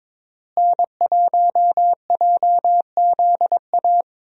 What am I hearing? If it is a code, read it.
Morse code, heard as a keyed tone, about 21 wpm, 700 Hz. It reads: N1JZA
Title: USA Call Signs - 5 Characters - Repeated 2x - 22wpm